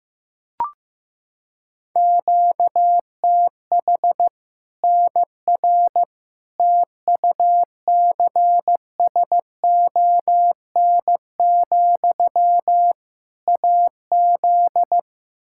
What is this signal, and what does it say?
Morse code, heard as a keyed tone, about 15 wpm, 700 Hz. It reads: QTH NR TUCSON, AZ
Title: K1USN SST QSO Practice 15wpm